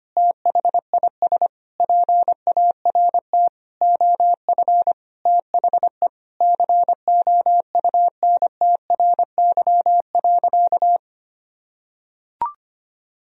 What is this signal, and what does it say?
Morse code, heard as a keyed tone, about 25 wpm, 700 Hz. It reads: THIS PART OF THE COUNTRY.
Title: Sentences from Top 300 Words and Abbreviations 25wpm